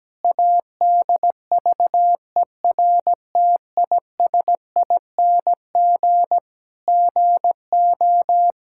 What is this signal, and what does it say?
Morse code, heard as a keyed tone, about 17 wpm, 700 Hz. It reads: ADVERTISING GO
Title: Sentences from Top 1000 Words - Review Part 6 - 17wpm